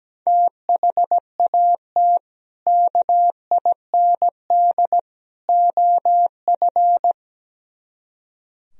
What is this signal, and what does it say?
Morse code, heard as a keyed tone, about 18 wpm, 700 Hz. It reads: THAT KIND OF
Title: Speed-Racing - Sets of 3 Words 17wpm